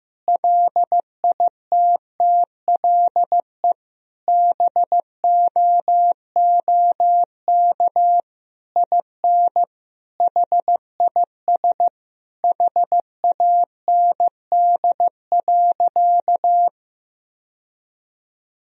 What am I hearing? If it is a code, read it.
Morse code, heard as a keyed tone, about 15 wpm, 700 Hz. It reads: LITTLE BOOK IN HIS HAND.
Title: Sentences from Top 1000 Words - Review Part 1 - 15wpm